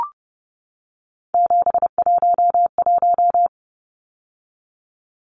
Morse code, heard as a keyed tone, about 30 wpm, 700 Hz. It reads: 711
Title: Numbers - 3-Digit 30wpm